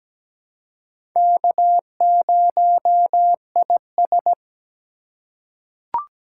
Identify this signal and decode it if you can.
Morse code, heard as a keyed tone, about 17 wpm, 700 Hz. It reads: K0IS
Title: USA Call Signs - 3x Character Spacing - 17wpm